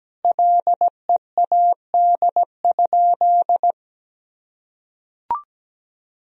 Morse code, heard as a keyed tone, about 17 wpm, 700 Hz. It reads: LEAD?
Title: Sentences from Top 1000 Words - Review Part 2 - 17wpm